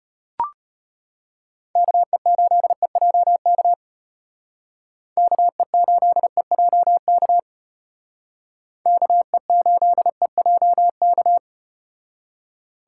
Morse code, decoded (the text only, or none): KE8EJK KE8EJK KE8EJK